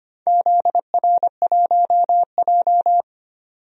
Morse code, heard as a keyed tone, about 25 wpm, 700 Hz. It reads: ZR1J
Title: DXCC-100 Call Signs - 4 Characters - 25wpm